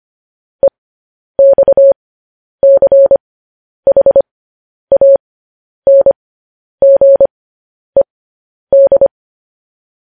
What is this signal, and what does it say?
Morse code, heard as a keyed tone, about 25 wpm, spaced as for 10 wpm, 550 Hz. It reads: EXCHANGED
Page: news-headlines-in-morse-code-at-10-wpm